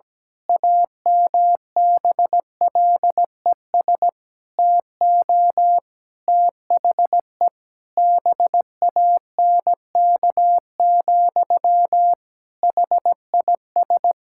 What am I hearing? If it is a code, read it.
Morse code, heard as a keyed tone, about 17 wpm, 700 Hz. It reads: AMBLES TO THE BANK, HIS
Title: Pangram Challenge v2 - 17wpm